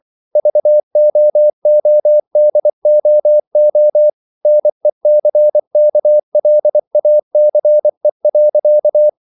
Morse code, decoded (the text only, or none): VOODOO NECKLACE.